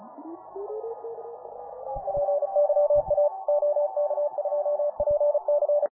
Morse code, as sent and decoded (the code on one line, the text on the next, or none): .--. -.-- .- / --- -.- ..--- .--. -.-.
PYA OK2PC